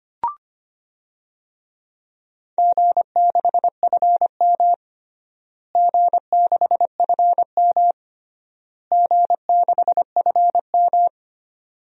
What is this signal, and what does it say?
Morse code, heard as a keyed tone, about 25 wpm, 700 Hz. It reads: G6FM G6FM G6FM
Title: DXCC-100 Call Signs - 4 Characters - Repeated 3x - 25wpm